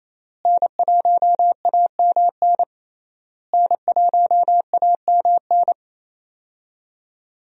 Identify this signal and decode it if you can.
Morse code, heard as a keyed tone, about 28 wpm, 700 Hz. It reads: N1AMN N1AMN
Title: USA Call Signs - 5 Characters - Repeated 3x - 28wpm